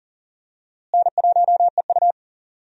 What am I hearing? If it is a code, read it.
Morse code, heard as a keyed tone, about 40 wpm, 700 Hz. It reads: N1EU
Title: USA Call Signs - 4 Characters - Repeated 2x - 40wpm